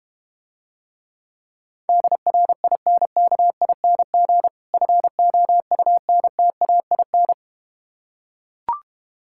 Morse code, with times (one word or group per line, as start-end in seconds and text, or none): 1.9-4.5 DRINKING
4.7-7.3 FOUNTAIN